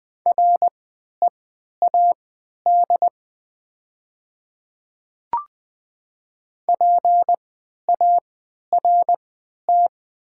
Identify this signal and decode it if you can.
Morse code, heard as a keyed tone, about 20 wpm, spaced as for 11 wpm, 700 Hz. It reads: READ PART
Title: 3x Character Spacing - Top 200 Words - 20wpm